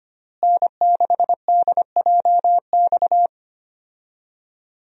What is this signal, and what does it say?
Morse code, heard as a keyed tone, about 25 wpm, 700 Hz. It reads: N6DJX
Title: USA Call Signs - 5 Characters - Repeated 2x - 25wpm